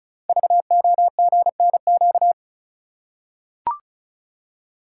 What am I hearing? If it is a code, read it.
Morse code, heard as a keyed tone, about 36 wpm, 690 Hz. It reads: VOGNQ
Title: Code Groups - 5 Letters - 35wpm